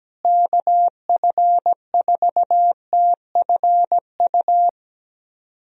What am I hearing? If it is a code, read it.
Morse code, heard as a keyed tone, about 17 wpm, 700 Hz. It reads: KF4TFU
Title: USA Call Signs - 6 Characters - Repeated 3x - 17wpm